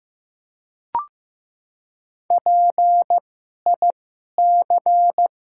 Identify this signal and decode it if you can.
Morse code, heard as a keyed tone, about 15 wpm, spaced as for 11 wpm, 700 Hz. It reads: PIC
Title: 2x Character Spacing - Top 300 Words - 15wpm